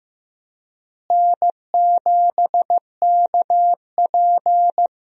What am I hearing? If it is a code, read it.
Morse code, heard as a keyed tone, about 15 wpm, 700 Hz. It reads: N7KP
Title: USA Call Signs - 4 Characters - Repeated 2x - 15wpm